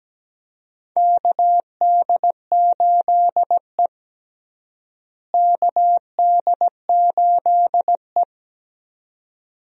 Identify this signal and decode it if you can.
Morse code, heard as a keyed tone, about 17 wpm, 700 Hz. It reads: KD8E KD8E
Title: USA Call Signs - 4 Characters - Repeated 2x - 17wpm